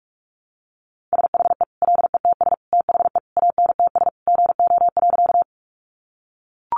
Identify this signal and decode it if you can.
Morse code, heard as a keyed tone, about 44 wpm, 700 Hz. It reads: SHE LETS THE ANTS GO.
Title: Easy Phrases 45wpm